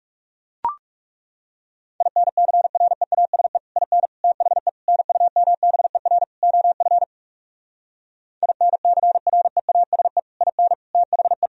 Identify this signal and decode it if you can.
Morse code, heard as a keyed tone, about 40 wpm, 700 Hz. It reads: INCREASE IN THE NUMBER OF INCREASE IN THE